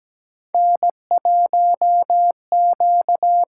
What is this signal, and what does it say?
Morse code, heard as a keyed tone, about 17 wpm, 700 Hz. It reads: N1Q
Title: USA Call Signs - 5 Characters - 8x Character Spacing - 17wpm